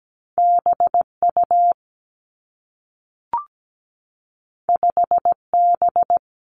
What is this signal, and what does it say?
Morse code, heard as a keyed tone, about 17 wpm, 700 Hz. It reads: BU 5B
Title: Two Challenging Letter-Number Combinations 17wpm